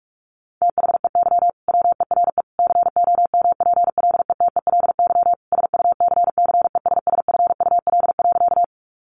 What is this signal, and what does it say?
Morse code, heard as a keyed tone, about 44 wpm, 700 Hz. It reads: THEY WERE COMPLETELY SUCCESSFUL.
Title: Sentences from Top 1100 Words 45wpm